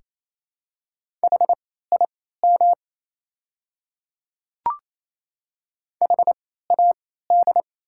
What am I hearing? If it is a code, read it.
Morse code, heard as a keyed tone, about 28 wpm, spaced as for 16 wpm, 700 Hz. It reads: HIM HAD
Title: 3x Character Spacing - Top 100 Words - 28wpm